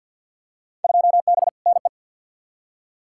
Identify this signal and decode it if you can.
Morse code, heard as a keyed tone, about 46 wpm, 700 Hz. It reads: 2B NE
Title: Field Day Exchanges Encore - 50wpm